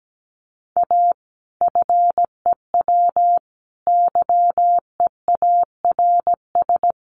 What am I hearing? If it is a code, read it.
Morse code, heard as a keyed tone, about 17 wpm, 700 Hz. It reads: A FEW YEARS
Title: Sets of 3 Words 17wpm